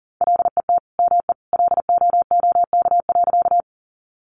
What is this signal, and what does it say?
Morse code, heard as a keyed tone, about 40 wpm, 700 Hz. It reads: LET ME LOOK.
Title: Sentences from Top 300 Words 40wpm